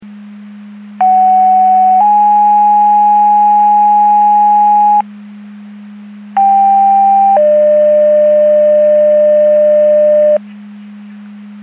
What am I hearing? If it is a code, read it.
Two-tone page, heard as a one-second tone, then a three-second tone, about 750 then 830 Hz, about 790 then 600 Hz. It is heard from 1.0 to 10.4 s.
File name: Portage / 01251 15-23 Portage 154.385 Paging